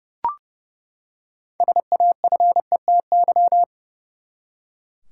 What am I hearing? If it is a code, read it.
Morse code, heard as a keyed tone, about 30 wpm, 700 Hz. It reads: SAFETY
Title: Six Letter Words - 30wpm